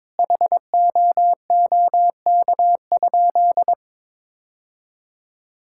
Morse code, heard as a keyed tone, about 21 wpm, 700 Hz. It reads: HOOK?
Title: Sentences from Top 1600 Words 22wpm